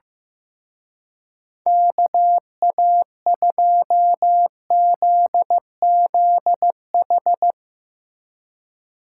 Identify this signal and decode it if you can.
Morse code, heard as a keyed tone, about 16 wpm, 700 Hz. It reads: KA2ZZH
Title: Speed Racing - USA 2x3 Call Signs 15wpm